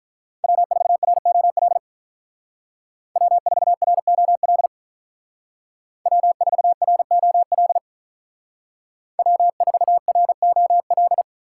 Morse code, heard as a keyed tone, about 43 wpm, 700 Hz. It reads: W4ROL W4ROL W4ROL W4ROL
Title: Speed Racing - USA Call Signs 35wpm